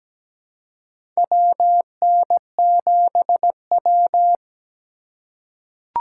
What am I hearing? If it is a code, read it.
Morse code, heard as a keyed tone, about 17 wpm, 700 Hz. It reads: WN7W
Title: USA Call Signs - 4x Character Spacing - 17wpm